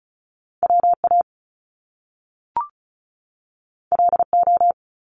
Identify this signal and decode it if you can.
Morse code, heard as a keyed tone, about 36 wpm, 700 Hz. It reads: WA LO
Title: Top Two Letter Combinations in Words 35wpm